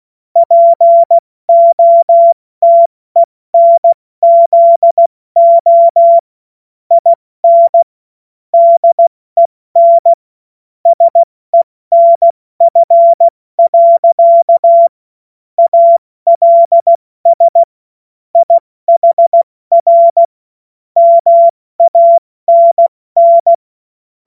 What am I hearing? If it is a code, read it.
Morse code, heard as a keyed tone, about 16 wpm, 680 Hz. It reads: POTENZO IN DEN SENF. ALS IHR MANN
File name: Morsekurs / Witze / Witz 09 April 2024 70 F80